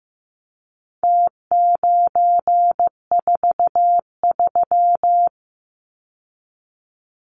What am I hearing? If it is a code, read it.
Morse code, heard as a keyed tone, about 15 wpm, 700 Hz. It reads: T943
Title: Numbers - Contest Style - 4-Digit 15wpm